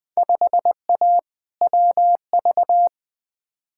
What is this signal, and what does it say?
Morse code, heard as a keyed tone, about 20 wpm, 700 Hz. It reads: 5A WV
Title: Field Day Exchanges - 20wpm